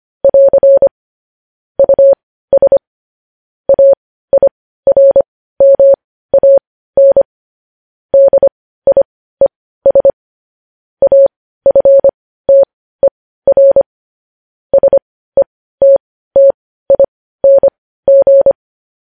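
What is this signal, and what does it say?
Morse code, heard as a keyed tone, about 25 wpm, spaced as for 15 wpm, 550 Hz. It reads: <AR> US AIRMAN DIES AFTER SETTING
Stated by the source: Morse code news headlines sent at 15 wpm.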